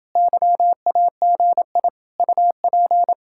Morse code, heard as a keyed tone, about 28 wpm, 700 Hz. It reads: YAGI UP